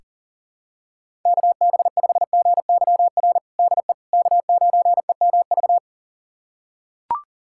Morse code, heard as a keyed tone, about 37 wpm, 700 Hz. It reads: KB5GYR DE K9EMV
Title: USA Callsign DE Callsign 40wpm